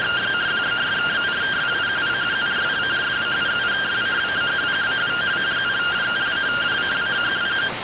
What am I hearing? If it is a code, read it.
Two-tone teleprinter signal, about 170 Hz shift.